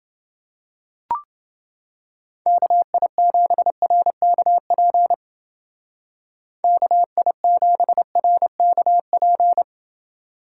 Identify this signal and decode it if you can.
Morse code, heard as a keyed tone, about 29 wpm, 700 Hz. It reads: KI7RKP KI7RKP